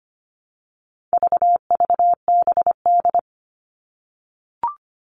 Morse code, heard as a keyed tone, about 25 wpm, 700 Hz. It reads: VVBD
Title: All Cut Numbers - 4-Digit 25wpm